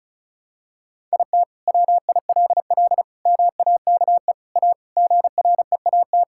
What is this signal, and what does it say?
Morse code, heard as a keyed tone, about 34 wpm, 700 Hz. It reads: IT WILL MAKE A GREAT
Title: Sentences from Top 1000 Words - Review Part 5 - 35wpm